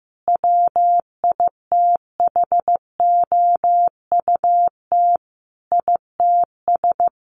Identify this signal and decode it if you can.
Morse code, heard as a keyed tone, about 15 wpm, 700 Hz. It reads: WITHOUT ITS
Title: Sentences from Top 1500 Words 15wpm